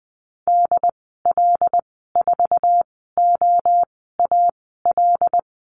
Morse code, heard as a keyed tone, about 20 wpm, spaced as for 14 wpm, 700 Hz. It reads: DL4OAL
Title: 122 Lesson 20g - Number 3 - Call Signs 20wpm